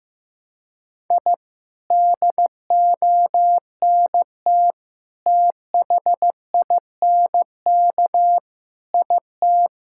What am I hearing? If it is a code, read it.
Morse code, heard as a keyed tone, about 14 wpm, 700 Hz. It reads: I DONT THINK IT
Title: Sentences from Top 1000 Words - Review Part 6 - 15wpm